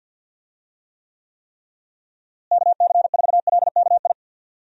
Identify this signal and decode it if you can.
Morse code, heard as a keyed tone, about 51 wpm, 700 Hz. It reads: KC4LKI
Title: USA Call Signs - 6 Characters - Repeated 3x - 50wpm